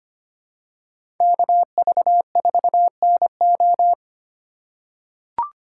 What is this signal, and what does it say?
Morse code, heard as a keyed tone, about 25 wpm, 700 Hz. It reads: KV4NO
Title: USA Call Signs - 5 Characters - 4x Character Spacing - 25wpm